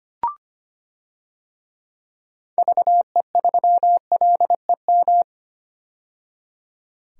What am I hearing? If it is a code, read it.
Morse code, heard as a keyed tone, about 25 wpm, 700 Hz. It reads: VE3LEM
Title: DXCC-100 Call Signs - 6 Characters - 25wpm